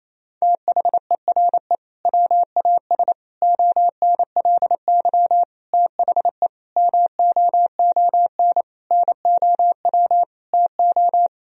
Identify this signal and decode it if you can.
Morse code, heard as a keyed tone, about 28 wpm, 700 Hz. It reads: THERE WAS ONLY THE MOON NOW TO
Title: Sentences from Top 1000 Words - Review Part 3 - 28wpm